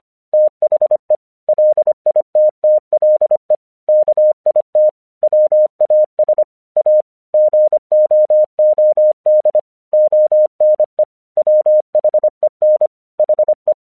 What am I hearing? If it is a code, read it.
Morse code, heard as a keyed tone, about 25 wpm, 610 Hz. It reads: THE LITTLE KIT WAS A GOOD ONE WHEN HE
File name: 1 English 25wpm / AQC27